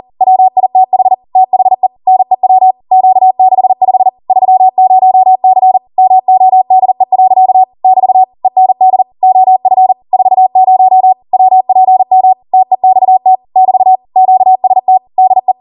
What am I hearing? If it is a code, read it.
Morse code, heard as a keyed tone, about 40 wpm, 750 Hz. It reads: WITH THE NEW Q65 30C MODE. <BT> END OF 40 WPM TEXT <BT> QST DE